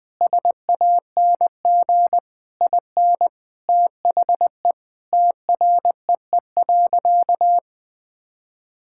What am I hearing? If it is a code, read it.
Morse code, heard as a keyed tone, about 21 wpm, 700 Hz. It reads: SANG IN THE TREE.
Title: Sentences with 2-Word Phrases - 20wpm